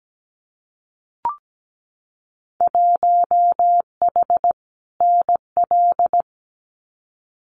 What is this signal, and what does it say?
Morse code, heard as a keyed tone, about 17 wpm, 700 Hz. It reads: 1H NL
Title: Winter Field Day Exchanges 17wpm